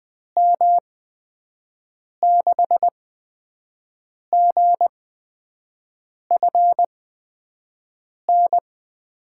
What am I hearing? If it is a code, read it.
Morse code, heard as a keyed tone, about 20 wpm, spaced as for 5 wpm, 700 Hz. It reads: M6GFN
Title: USA Call Signs - 6 Characters - 8x Character Spacing - 20wpm